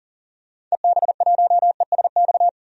Morse code, heard as a keyed tone, about 37 wpm, 700 Hz. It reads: EB1ESX
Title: DXCC-100 Call Signs - 6 Characters - 40wpm